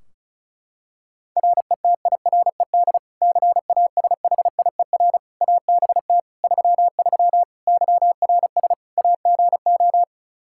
Morse code, heard as a keyed tone, about 36 wpm, 700 Hz. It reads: RETIRED CASHIER ABT 33 YRS AGO
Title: Practice QSO - Exchange 3 - More Info - 35wpm